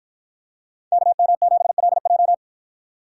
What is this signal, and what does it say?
Morse code, heard as a keyed tone, about 53 wpm, 700 Hz. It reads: KM7LJ